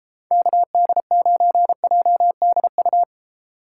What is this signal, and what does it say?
Morse code, heard as a keyed tone, about 30 wpm, 700 Hz. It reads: KD9JDU